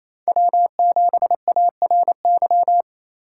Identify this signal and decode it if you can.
Morse code, heard as a keyed tone, about 28 wpm, 700 Hz. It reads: W7ARY
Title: USA Call Signs - 5 Characters - 4x Character Spacing - 28wpm